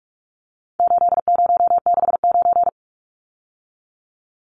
Morse code, heard as a keyed tone, about 44 wpm, 700 Hz. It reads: 8069
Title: Numbers - 4-Digit 45wpm